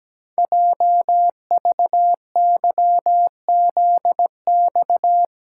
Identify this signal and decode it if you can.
Morse code, heard as a keyed tone, about 17 wpm, 700 Hz. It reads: JVYZX
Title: Code Groups - 5 Letters - 17wpm